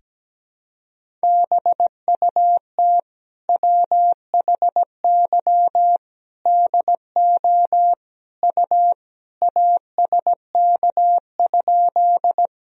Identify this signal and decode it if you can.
Morse code, heard as a keyed tone, about 17 wpm, 700 Hz. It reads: BUT WHY DO U ASK?
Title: Sentences from Top 200 Words and Abbreviations 17wpm